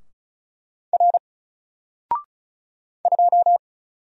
Morse code, heard as a keyed tone, about 36 wpm, 700 Hz. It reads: R 2
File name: Single-Letter-Number-35wpm